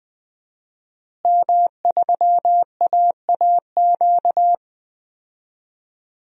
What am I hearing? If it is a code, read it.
Morse code, heard as a keyed tone, about 21 wpm, 700 Hz. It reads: M3AAQ
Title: DXCC-100 Call Signs - 5 Characters - Repeated 3x - 20wpm